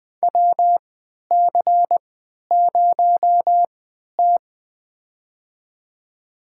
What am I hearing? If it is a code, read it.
Morse code, heard as a keyed tone, about 20 wpm, spaced as for 11 wpm, 700 Hz. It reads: WC0T
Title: USA Call Signs - 4 Characters - 3x Character Spacing - 20wpm